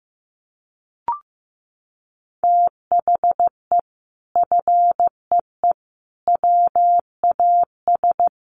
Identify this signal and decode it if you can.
Morse code, heard as a keyed tone, about 14 wpm, 700 Hz. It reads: THE FEE WAS
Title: Sentences from Top 1500 Words 15wpm